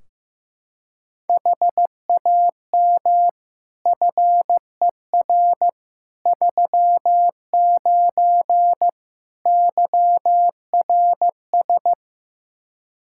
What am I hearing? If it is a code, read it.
Morse code, heard as a keyed tone, about 16 wpm, 700 Hz. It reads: HAM FER 39 YRS
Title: Practice QSO - Exchange 3 - More Info - 15wpm